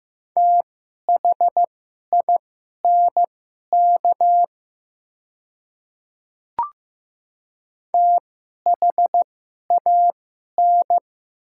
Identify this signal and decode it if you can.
Morse code, heard as a keyed tone, about 15 wpm, spaced as for 11 wpm, 700 Hz. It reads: THINK THAN
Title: 2x Character Spacing - Top 200 Words - 15wpm